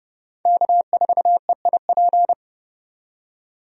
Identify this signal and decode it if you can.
Morse code, heard as a keyed tone, about 30 wpm, 700 Hz. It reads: K4EIP
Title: USA Call Signs - 5 Characters - 30wpm